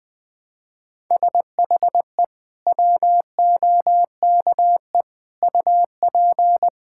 Morse code, heard as a keyed tone, about 20 wpm, 700 Hz. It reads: SHE WOKE UP
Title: Sentences with 8-Word Phrases - Spoken Beginning - 20wpm